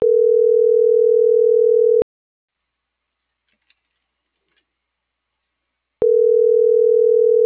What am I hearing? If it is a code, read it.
Ringback tone (North America), 440 + 480 Hz.